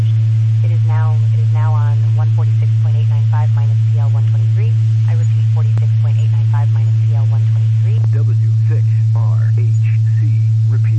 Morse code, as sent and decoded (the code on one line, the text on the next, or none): none